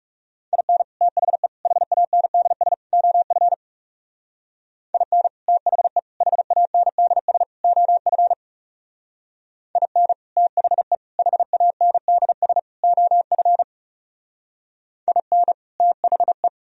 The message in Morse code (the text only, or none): IN THE HANDS OF IN THE HANDS OF IN THE HANDS OF IN THE